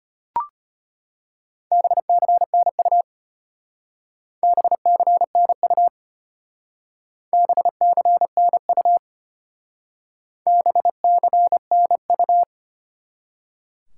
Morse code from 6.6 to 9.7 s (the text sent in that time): BCNU